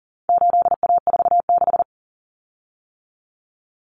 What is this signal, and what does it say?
Morse code, heard as a keyed tone, about 40 wpm, 700 Hz. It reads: 8A46
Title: Common Cut Numbers - 4-Digit 40wpm